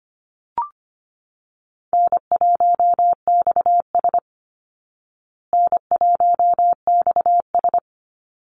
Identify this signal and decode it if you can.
Morse code, heard as a keyed tone, about 25 wpm, 700 Hz. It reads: N1XS N1XS
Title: Easy Call Signs Repeated 3x - 25wpm